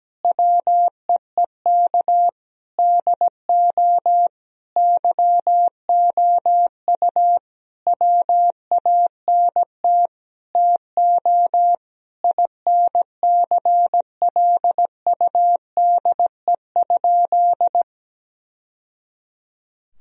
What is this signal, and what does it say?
Morse code, heard as a keyed tone, about 17 wpm, 700 Hz. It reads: WEEK DO YOU WANT TO INCLUDE?
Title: Sentences from Top 1000 Words - Review Part 3 - 17wpm